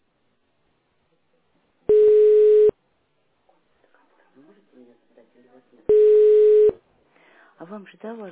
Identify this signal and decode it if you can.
Ringback tone (425 Hz, 1 s on, 3 s off), with speech.